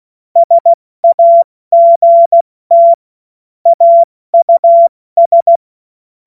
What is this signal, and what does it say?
Morse code, heard as a keyed tone, about 16 wpm, 680 Hz. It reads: SAGT AUS